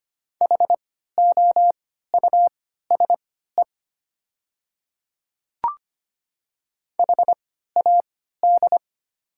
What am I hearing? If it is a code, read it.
Morse code, heard as a keyed tone, about 25 wpm, spaced as for 14 wpm, 700 Hz. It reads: HOUSE HAD
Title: 3x Character Spacing - Top 300 Words - 25wpm